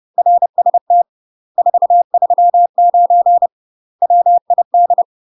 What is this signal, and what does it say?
Morse code, heard as a keyed tone, about 30 wpm, 700 Hz. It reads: RST 439 WID
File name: UK MASTER QSOs TRAINING / 30WPM / UK MASTER QSOs 0051